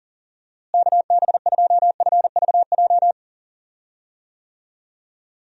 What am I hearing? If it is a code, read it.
Morse code, heard as a keyed tone, about 40 wpm, 700 Hz. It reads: KB2FVJ
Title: USA Call Signs - 6 Characters - Repeated 2x - 40wpm